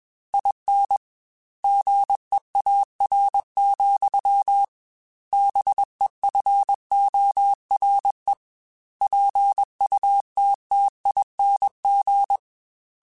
Morse code, heard as a keyed tone, about 21 wpm, 800 Hz. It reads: IN GEAR, BEFORE PUTTING